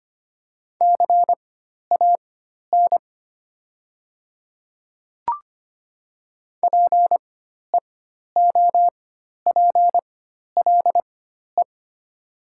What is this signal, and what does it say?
Morse code, heard as a keyed tone, about 25 wpm, spaced as for 12 wpm, 700 Hz. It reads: CAN PEOPLE
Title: 4x Character Spacing - Top 200 Words - 25wpm